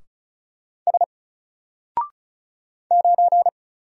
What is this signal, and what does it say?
Morse code, heard as a keyed tone, about 36 wpm, 700 Hz. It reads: S 9
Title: Single Letter-Number Alt 35wpm